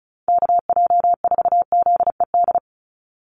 Morse code, heard as a keyed tone, about 34 wpm, 700 Hz. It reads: KJ4ZED
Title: USA Call Signs 35wpm